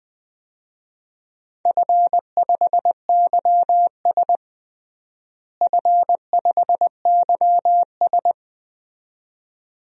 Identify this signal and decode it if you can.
Morse code, heard as a keyed tone, about 20 wpm, 700 Hz. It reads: F5YS F5YS
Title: DXCC-100 Call Signs - 4 Characters - Repeated 2x - 20wpm